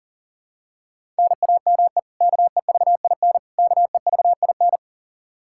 Morse code, heard as a keyed tone, about 40 wpm, 700 Hz. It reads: NAME KEVIN KEVIN
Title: Practice QSO - Exchange 1 - Beginning - 40wpm